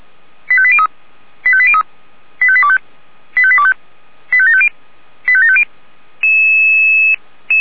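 Five-tone selective call (ZVEI), heard as quick runs of five tones, tones of about 70 ms. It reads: 85803 85803 85825 85825 85750 85750
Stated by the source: ZVEI five-tone selective call received on an SDR.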